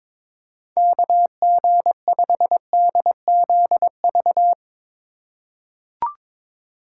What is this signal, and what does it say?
Morse code, heard as a keyed tone, about 22 wpm, 700 Hz. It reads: KG5DZV
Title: USA Call Signs - 3x Character Spacing - 22wpm